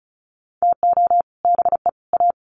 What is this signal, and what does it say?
Morse code, heard as a keyed tone, about 34 wpm, 700 Hz. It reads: TO BE A
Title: Sets of 3 Words 35wpm